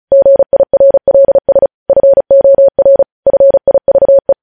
Morse code, heard as a keyed tone, about 36 wpm, 550 Hz. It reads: GIRLS FOR FIVE
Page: news headlines in morse code at 35 wpm